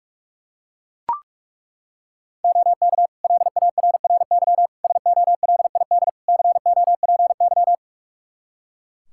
Morse code, heard as a keyed tone, about 42 wpm, 700 Hz. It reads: OK LARRY SOLID COPY
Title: Practice QSO - Exchange 3 - More Info - 45wpm